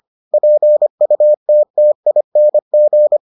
Morse code, heard as a keyed tone, about 25 wpm, 600 Hz. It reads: PUTTING